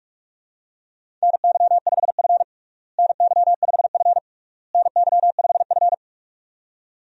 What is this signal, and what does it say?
Morse code, heard as a keyed tone, about 41 wpm, 700 Hz. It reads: NY5F NY5F NY5F
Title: USA Call Signs - 4 Characters - Repeated 3x - 45wpm